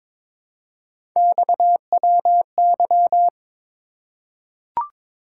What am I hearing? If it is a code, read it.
Morse code, heard as a keyed tone, about 23 wpm, 700 Hz. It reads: XWY
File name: practice / 22wpm / Three Letter-Number Combinations - 22wpm